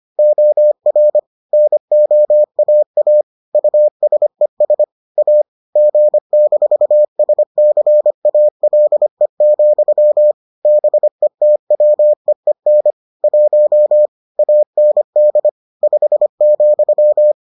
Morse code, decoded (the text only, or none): OR NOAA USES A G-SCALE, BETWEEN 1 AND 5,